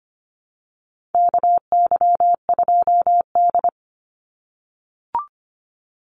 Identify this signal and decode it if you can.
Morse code, heard as a keyed tone, about 25 wpm, 700 Hz. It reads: KY2D
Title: Call Signs - 25wpm